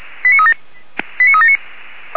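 Five-tone selective call (ZVEI), heard as quick runs of five tones, tones of about 80 ms. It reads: 89379
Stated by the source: ZVEI five-tone selective call received by SDR.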